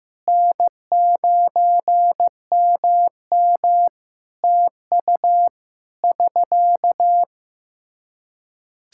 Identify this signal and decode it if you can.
Morse code, heard as a keyed tone, about 15 wpm, 700 Hz. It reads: N9MM TU <SK>
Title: K1USN SST QSO Practice 15wpm